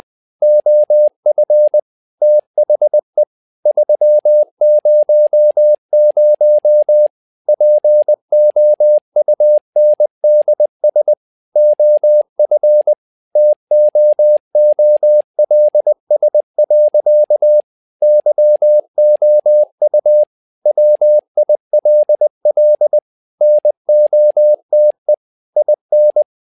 Morse code, heard as a keyed tone, about 20 wpm, 600 Hz. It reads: OF THE 300 POUNDS OF TOOLS. YOU WILL NOTE IN
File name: learning center / tower trouble / tower trouble 20wpm